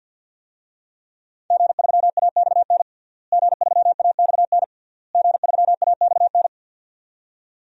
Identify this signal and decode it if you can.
Morse code, heard as a keyed tone, about 50 wpm, 700 Hz. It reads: G3AXN G3AXN G3AXN
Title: DXCC-100 Call Signs - 5 Characters - Repeated 3x - 50wpm